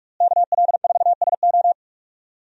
Morse code, heard as a keyed tone, about 43 wpm, 700 Hz. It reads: KL4SO
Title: USA Call Signs - 5 Characters - Repeated 3x - 45wpm